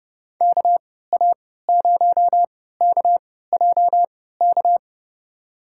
Morse code, heard as a keyed tone, about 29 wpm, spaced as for 17 wpm, 700 Hz. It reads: KA0KJK
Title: USA Call Signs - 6 Characters - 3x Character Spacing - 30wpm